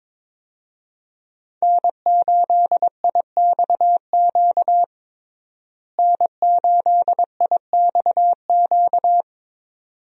Morse code, heard as a keyed tone, about 21 wpm, 700 Hz. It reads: N8IXQ N8IXQ
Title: USA Call Signs - 5 Characters - Repeated 2x - 22wpm